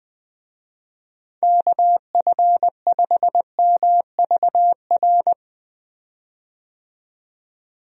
Morse code, heard as a keyed tone, about 20 wpm, 700 Hz. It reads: KF5MVR
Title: USA Call Signs - 6 Characters - 20wpm